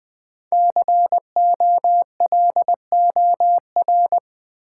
Morse code, heard as a keyed tone, about 20 wpm, 700 Hz. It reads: COLOR
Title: Sentences from Top 1000 Words - Review Part 2 - New Word Drill - 20wpm